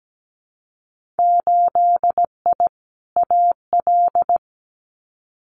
Morse code, heard as a keyed tone, about 17 wpm, 700 Hz. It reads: 8I AL
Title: Winter Field Day Exchanges Encore 17wpm